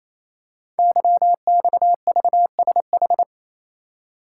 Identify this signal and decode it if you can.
Morse code, heard as a keyed tone, about 28 wpm, 700 Hz. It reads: YXVSH
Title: Code Groups - 5 Letter:Numbers - 28wpm